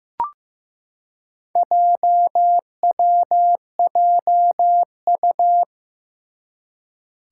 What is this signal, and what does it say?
Morse code, heard as a keyed tone, about 15 wpm, 700 Hz. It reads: JWJU
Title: Four Challenging Letter-Number Combinations 15wpm